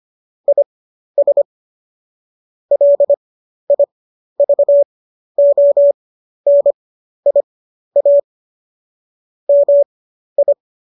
Morse code, heard as a keyed tone, about 25 wpm, spaced as for 12 wpm, 570 Hz. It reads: IS LIVONIA MI